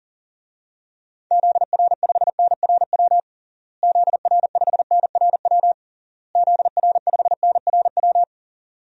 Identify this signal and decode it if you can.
Morse code, heard as a keyed tone, about 37 wpm, 700 Hz. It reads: ZR5NRW ZR5NRW ZR5NRW
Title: DXCC-100 Call Signs - 6 Characters - Repeated 3x - 40wpm